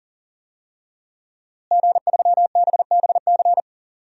Morse code, heard as a keyed tone, about 40 wpm, 700 Hz. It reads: G3BBC
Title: DXCC-100 Call Signs - 5 Characters - Repeated 3x - 40wpm